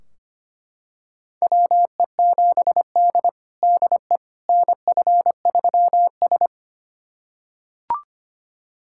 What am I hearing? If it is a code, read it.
Morse code, heard as a keyed tone, about 25 wpm, 700 Hz. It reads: WE7D DE NF3S
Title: USA Callsign DE Callsign - 4 Characters - 25wpm